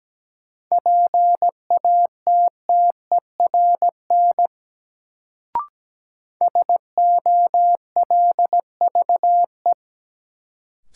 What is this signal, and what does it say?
Morse code, heard as a keyed tone, about 17 wpm, 700 Hz. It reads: PATTERN SOLVE
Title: Sentences from Top 1000 Words - New Word Drill - 17wpm